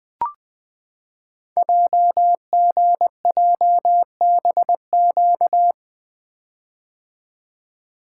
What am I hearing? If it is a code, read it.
Morse code, heard as a keyed tone, about 20 wpm, 700 Hz. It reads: JGJBQ
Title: Code Groups - 5 Letters - 20wpm